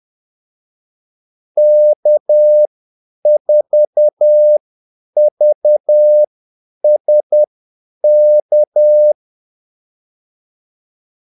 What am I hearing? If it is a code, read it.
Morse code, heard as a keyed tone, about 10 wpm, spaced as for 8 wpm, 600 Hz. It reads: K4VSK
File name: Callsigns-10-8wpm